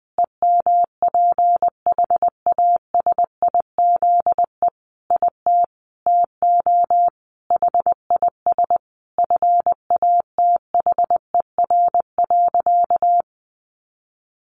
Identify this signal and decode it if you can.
Morse code, heard as a keyed tone, about 20 wpm, 700 Hz. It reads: EMPHASIZE IT TO HIS FATHER.
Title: Sentences from Top 1500 Words 20wpm